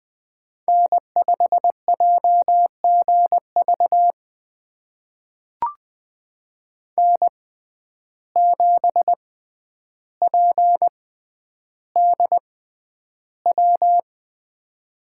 Morse code, heard as a keyed tone, about 20 wpm, spaced as for 10 wpm, 700 Hz. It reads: N5JGV N7PDW
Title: USA Call Signs - 5 Characters - 6x Character Spacing - 20wpm